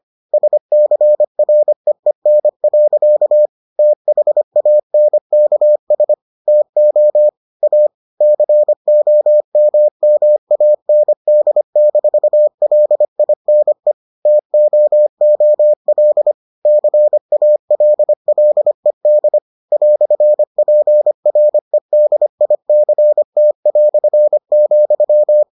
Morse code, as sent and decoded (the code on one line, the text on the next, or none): ... -.-. .-. . -. .-.-.- / - .... .- -. -.- ... / - --- / .- / -.-. --- -- -- .- -. -.. -....- .-.. .. -. / - --- --- .-.. / -.-. .- .-.. .-.. -.. / .-..-. .--. .-. -.. .. -.-. - .-..-. --..--
SCREEN. THANKS TO A COMMAND-LINE TOOL CALLED "PREDICT",